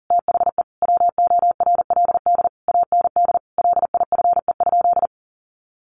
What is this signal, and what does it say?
Morse code, heard as a keyed tone, about 40 wpm, 700 Hz. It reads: THE WORLD AND LIFE?
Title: Sentences from Top 1300 Words 40wpm